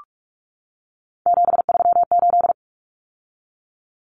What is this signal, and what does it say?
Morse code, heard as a keyed tone, about 44 wpm, 700 Hz. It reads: 738
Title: Numbers - 3-Digit 45wpm